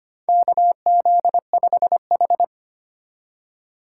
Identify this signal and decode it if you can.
Morse code, heard as a keyed tone, about 25 wpm, 700 Hz. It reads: KZ5H
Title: USA Call Signs - 4 Characters - 2x Character Spacing - 25wpm